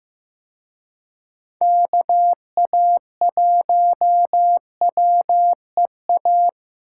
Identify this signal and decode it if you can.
Morse code, heard as a keyed tone, about 18 wpm, 700 Hz. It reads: KA1WEA
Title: Speed Racing - USA 2x3 Call Signs 15wpm